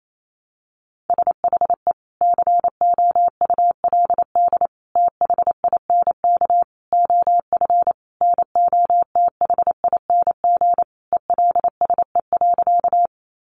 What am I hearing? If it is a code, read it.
Morse code, heard as a keyed tone, about 28 wpm, 700 Hz. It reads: SHE COULD THINK OF NOTHING ELSE.
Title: Sentences from Top 800 Words 28wpm